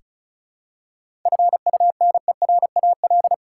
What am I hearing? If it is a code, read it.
Morse code, heard as a keyed tone, about 36 wpm, 700 Hz. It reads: FUNERAL
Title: Seven Letter Words - 35wpm